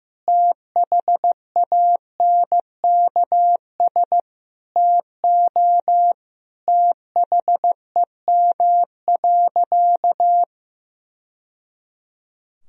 Morse code, heard as a keyed tone, about 15 wpm, 700 Hz. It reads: THANKS TO THEM.
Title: Sentences from Top 1000 Words - Review Part 3 - 15wpm